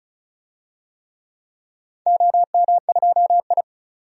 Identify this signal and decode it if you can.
Morse code, heard as a keyed tone, about 36 wpm, 700 Hz. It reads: OM2I
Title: DXCC-100 Call Signs - 4 Characters - 35wpm